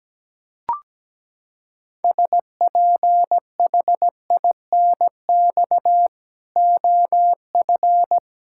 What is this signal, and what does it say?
Morse code, heard as a keyed tone, about 17 wpm, 700 Hz. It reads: SPHINX OF
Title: Pangram Challenge v2 - 17wpm